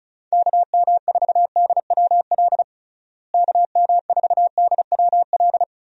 Morse code, heard as a keyed tone, about 34 wpm, 700 Hz. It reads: KM4DWL KM4DWL
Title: USA Call Signs - 6 Characters - Repeated 2x - 35wpm